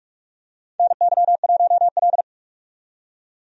Morse code, heard as a keyed tone, about 46 wpm, 700 Hz. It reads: NY1L
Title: USA Call Signs - 4 Characters - 45wpm